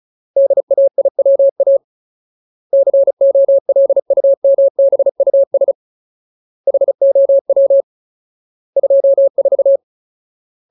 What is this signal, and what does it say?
Morse code, heard as a keyed tone, about 34 wpm, 550 Hz. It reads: DAIWA COLUMBUS HOW 24
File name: Ultimate Word List / 35WPM Ultimate